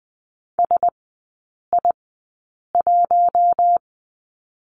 Morse code, heard as a keyed tone, about 20 wpm, spaced as for 8 wpm, 700 Hz. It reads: SI1
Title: 074 Lesson 13e - Letter L - Three Characters 20wpm